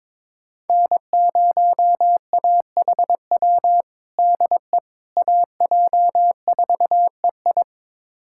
Morse code, heard as a keyed tone, about 21 wpm, 700 Hz. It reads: N0AHW DE AJ4EI
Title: USA Callsign DE Callsign - 5 Characters - 22wpm